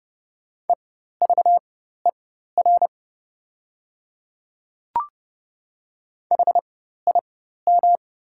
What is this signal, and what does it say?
Morse code, heard as a keyed tone, about 30 wpm, spaced as for 14 wpm, 690 Hz. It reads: EVER HIM
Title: 4x Character Spacing - Top 500 Words - 30wpm